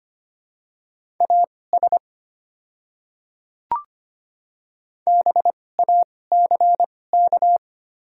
Morse code, heard as a keyed tone, about 25 wpm, spaced as for 18 wpm, 700 Hz. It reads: AS BACK
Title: 2x Character Spacing - Top 300 Words - 25wpm